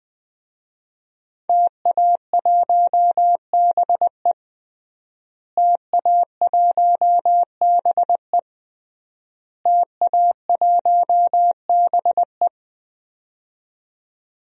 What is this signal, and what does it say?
Morse code, heard as a keyed tone, about 21 wpm, 700 Hz. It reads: TA1BE TA1BE TA1BE
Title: DXCC-100 Call Signs - 5 Characters - Repeated 3x - 20wpm